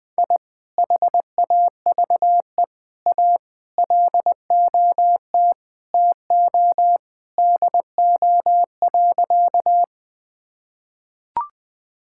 Morse code, heard as a keyed tone, about 20 wpm, 700 Hz. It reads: I HAVE A LOT TO DO.
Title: Sentences with 2-Word Phrases - 20wpm